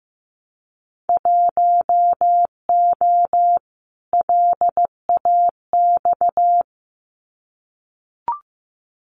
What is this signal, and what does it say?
Morse code, heard as a keyed tone, about 15 wpm, 700 Hz. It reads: 1O LAX
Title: Winter Field Day Exchanges 15wpm